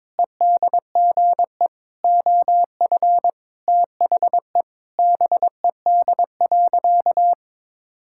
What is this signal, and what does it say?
Morse code, heard as a keyed tone, about 22 wpm, 700 Hz. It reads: EDGE OF THE BED.
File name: practice / 22wpm / Sentences with 4-Word Phrases - 22wpm